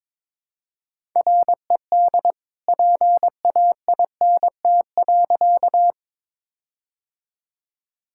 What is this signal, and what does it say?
Morse code, heard as a keyed tone, about 22 wpm, 700 Hz. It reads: RED PAINT.
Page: morse-code-ninja-practice-22wpm